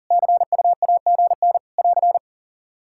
Morse code, heard as a keyed tone, about 40 wpm, 700 Hz. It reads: CUAGN <AR>
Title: K1USN SST QSO Practice 40wpm